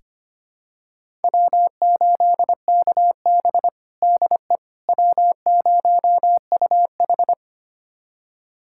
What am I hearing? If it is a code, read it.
Morse code, heard as a keyed tone, about 25 wpm, 700 Hz. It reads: W8KB DE W0UH
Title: USA Callsign DE Callsign - 4 Characters - 25wpm